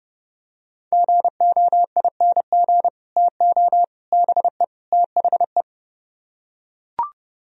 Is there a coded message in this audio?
Yes: Morse code from 0.9 to 5.6 s, heard as a keyed tone, about 30 wpm, 700 Hz. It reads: GOING TO BE THE